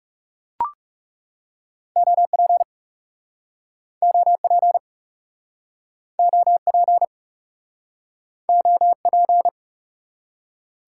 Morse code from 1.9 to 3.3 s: OP

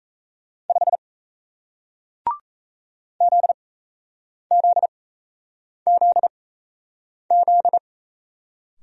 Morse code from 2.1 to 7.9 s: ZZZZ

